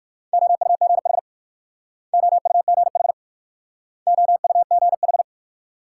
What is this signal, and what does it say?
Morse code, heard as a keyed tone, about 56 wpm, 690 Hz. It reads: OUGH OUGH OUGH